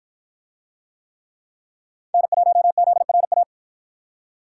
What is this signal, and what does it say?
Morse code, heard as a keyed tone, about 48 wpm, 700 Hz. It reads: N1ZRA